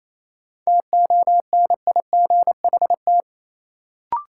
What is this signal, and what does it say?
Morse code, heard as a keyed tone, about 28 wpm, 700 Hz. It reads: TONIGHT
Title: Sentences from Top 600 Words - New Word Drill - 28wpm